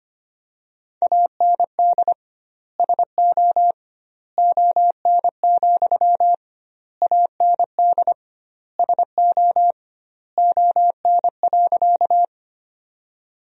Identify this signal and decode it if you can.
Morse code, heard as a keyed tone, about 25 wpm, 700 Hz. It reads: AND SO ON, AND SO ON.
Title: 2x Word-Spacing - Sentences from Top 100 Words - 25wpm